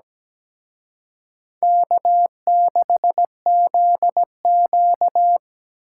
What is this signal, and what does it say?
Morse code, heard as a keyed tone, about 20 wpm, 700 Hz. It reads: K6ZQ
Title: Speed Racing - USA 1x2 Call Signs 17wpm